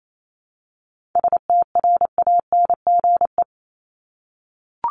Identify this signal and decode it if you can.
Morse code, heard as a keyed tone, about 28 wpm, 700 Hz. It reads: STRANGE
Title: Top 1501-1750 Words 28wpm